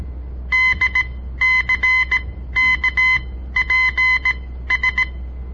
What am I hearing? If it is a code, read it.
Morse code, heard as a keyed tone, about 17 wpm, spaced as for 13 wpm, 2050 Hz. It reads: DCKPS
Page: On the air Morse code practice from the Southern Montana Amatuer Radio Association; all code, no read back